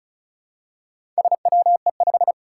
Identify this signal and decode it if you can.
Morse code, heard as a keyed tone, about 36 wpm, 700 Hz. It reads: SWE5
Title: Four Challenging Letter-Number Combinations 35wpm